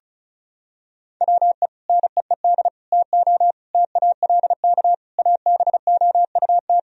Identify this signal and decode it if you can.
Morse code, heard as a keyed tone, about 34 wpm, 700 Hz. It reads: WE NEED TO TALK ABOUT
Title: Sentences from Top 1000 Words - Review Part 4 - 35wpm